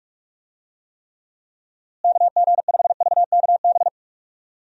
Morse code, heard as a keyed tone, about 47 wpm, 700 Hz. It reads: KG5VKB
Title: USA Call Signs - 6 Characters - Repeated 3x - 45wpm